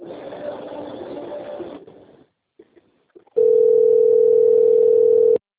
Ringback tone (North America), 440 + 480 Hz.